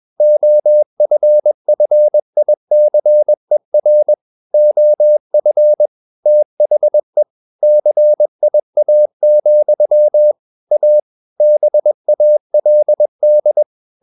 Morse code, heard as a keyed tone, about 21 wpm, 600 Hz. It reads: OFFICER OF THE CIA, A BALD
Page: CW med Gnister